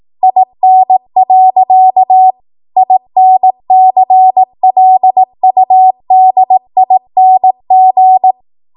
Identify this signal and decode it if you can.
Morse code, heard as a keyed tone, about 18 wpm, 750 Hz. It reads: IN. INCLUDING
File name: Morse / 18WPM